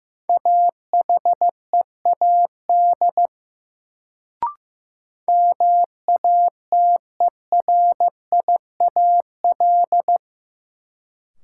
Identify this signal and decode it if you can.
Morse code, heard as a keyed tone, about 15 wpm, 700 Hz. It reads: AHEAD MATERIAL
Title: Sentences from Top 1100 Words - New Word Drill Alt - 15wpm